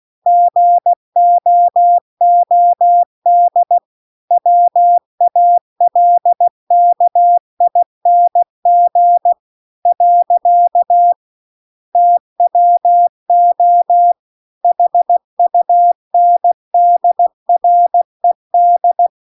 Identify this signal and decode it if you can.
Morse code, heard as a keyed tone, about 16 wpm, 700 Hz. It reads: GOOD WALKING . TWO HUNDRED